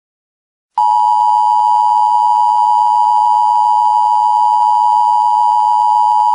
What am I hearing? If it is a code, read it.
Emergency Alert System (SAME) tones (the 853 and 960 Hz attention tone). It is heard from 0.7 to 6.4 s.